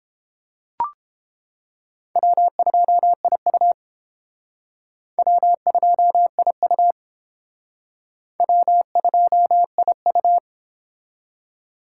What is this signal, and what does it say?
Morse code, heard as a keyed tone, about 29 wpm, 700 Hz. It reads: W2IU W2IU W2IU